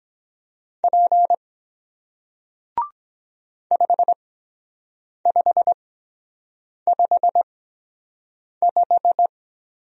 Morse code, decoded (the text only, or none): P 5 5 5 5